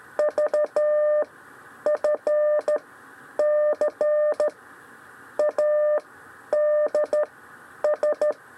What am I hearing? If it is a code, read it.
Morse code, heard as a keyed tone, about 13 wpm, 600 Hz. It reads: V F C A D S
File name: 6 lezione cw 1